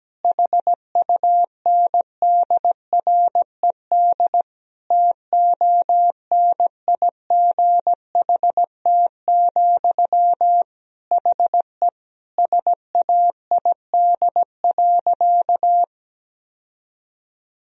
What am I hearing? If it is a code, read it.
Morse code, heard as a keyed tone, about 17 wpm, 700 Hz. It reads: HUNDRED TONIGHT, HE SAID.
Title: Sentences from Top 1000 Words - Review Part 2 - 17wpm